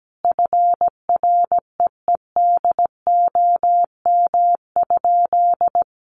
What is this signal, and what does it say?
Morse code, heard as a keyed tone, about 17 wpm, 700 Hz. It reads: FREEDOM?
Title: Sentences from Top 1500 Words 17wpm